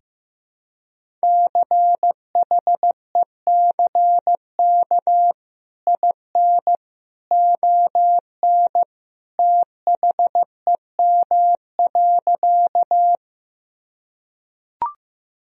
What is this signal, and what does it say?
Morse code, heard as a keyed tone, about 16 wpm, 700 Hz. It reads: CHECK IN ON THEM.
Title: Sentences from Top 1000 Words - Review Part 3 - 15wpm